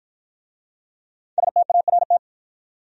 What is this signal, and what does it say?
Morse code, heard as a keyed tone, about 53 wpm, 700 Hz. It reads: START